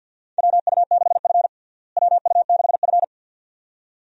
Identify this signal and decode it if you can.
Morse code, heard as a keyed tone, about 47 wpm, 690 Hz. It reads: WU6F WU6F
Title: USA Call Signs - 4 Characters - Repeated 2x - 50wpm